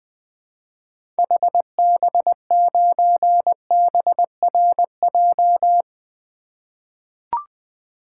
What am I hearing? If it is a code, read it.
Morse code, heard as a keyed tone, about 20 wpm, 700 Hz. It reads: HB9BRJ
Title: 2023 SOTA Activator Call Signs - 20wpm